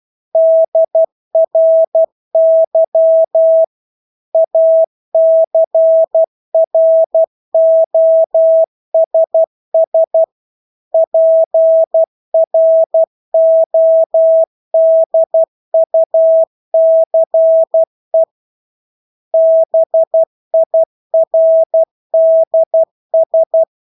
Morse code, heard as a keyed tone, about 12 wpm, 650 Hz. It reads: DRY ACROSS PRODUCE BIRDS
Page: Engelske ord 12 wpm